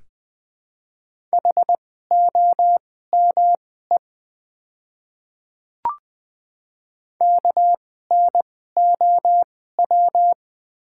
Morse code, heard as a keyed tone, about 20 wpm, spaced as for 14 wpm, 700 Hz. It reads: HOME KNOW
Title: 2x Character Spacing - Top 200 Words - 20wpm